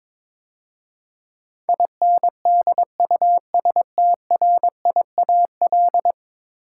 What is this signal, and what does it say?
Morse code, heard as a keyed tone, about 21 wpm, 700 Hz. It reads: INDUSTRIAL
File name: practice / 22wpm / 10 Letter Words - First 3 Letters Given - 22wpm